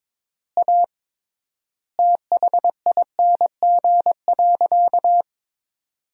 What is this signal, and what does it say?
Morse code, heard as a keyed tone, about 22 wpm, 700 Hz. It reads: A THING.
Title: 3x Word-Spacing - Sentences from Top 200 Words - 22wpm